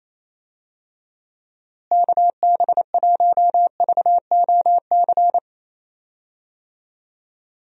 Morse code, heard as a keyed tone, about 28 wpm, 700 Hz. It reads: KB1VOC
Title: USA Call Signs - 6 Characters - Repeated 2x - 28wpm